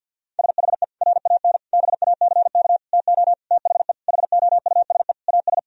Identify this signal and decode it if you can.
Morse code, heard as a keyed tone, about 50 wpm, 700 Hz. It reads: SHE RAN BACK TO THE HOUSE AS
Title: Sentences from Top 1000 Words - Review Part 3 - 50wpm